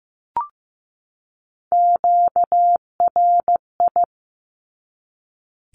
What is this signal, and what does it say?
Morse code, heard as a keyed tone, about 15 wpm, 700 Hz. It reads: QRI
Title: Q-Codes 15wpm